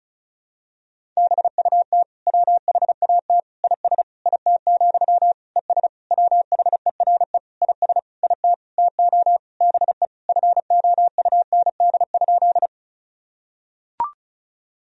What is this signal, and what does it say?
Morse code, heard as a keyed tone, about 35 wpm, 700 Hz. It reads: BUT WHAT IS IT, ES WHERE IS IT TO BE FOUND?
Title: Sentences from Top 300 Words and Abbreviations 35wpm